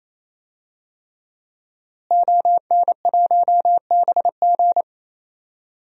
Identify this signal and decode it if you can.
Morse code, heard as a keyed tone, about 28 wpm, 700 Hz. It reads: ON1BG
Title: DXCC-100 Call Signs - 5 Characters - Repeated 3x - 28wpm